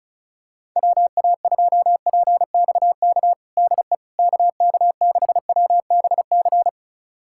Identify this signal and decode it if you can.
Morse code, heard as a keyed tone, about 34 wpm, 700 Hz. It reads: WA2PXK DE KK6WBC
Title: USA Callsign DE Callsign - 6 Characters - 35wpm